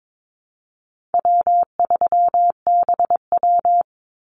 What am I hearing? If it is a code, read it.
Morse code, heard as a keyed tone, about 22 wpm, 700 Hz. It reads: W3BW
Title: Easy Call Signs Repeated 3x - 22wpm